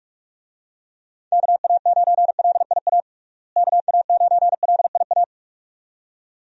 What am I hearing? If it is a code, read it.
Morse code, heard as a keyed tone, about 43 wpm, 700 Hz. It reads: KA9LIA KA9LIA
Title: USA Call Signs - 6 Characters - Repeated 2x - 45wpm